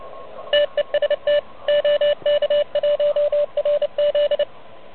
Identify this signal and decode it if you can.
Morse code, heard as a keyed tone, about 30 wpm, 590 Hz. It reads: TEST OK1RZ
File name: truSDX / MWC29.8